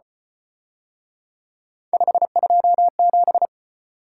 Morse code, heard as a keyed tone, about 32 wpm, 700 Hz. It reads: 527